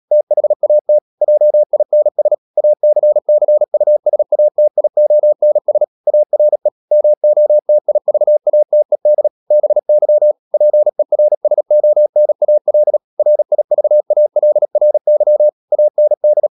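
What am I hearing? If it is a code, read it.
Morse code, heard as a keyed tone, about 37 wpm, 600 Hz. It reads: THAT JINS ACCUSATIONS ARE MOTIVATED BY PERSONAL RIVALRY AND
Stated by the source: Morse code practice at 35–39 wpm sending English text of Ken Follett's novel Never, from CW med Gnister.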